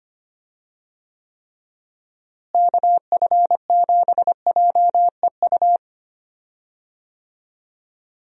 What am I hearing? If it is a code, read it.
Morse code, heard as a keyed tone, about 25 wpm, 700 Hz. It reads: KF7JEU
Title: USA Call Signs - 6 Characters - Repeated 3x - 25wpm